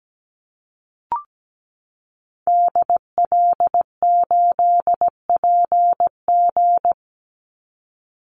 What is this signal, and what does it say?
Morse code, heard as a keyed tone, about 17 wpm, 700 Hz. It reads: DL8PG
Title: Call Signs - 17wpm